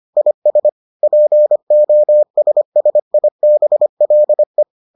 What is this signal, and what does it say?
Morse code, heard as a keyed tone, about 25 wpm, 600 Hz. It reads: IS POSSIBLE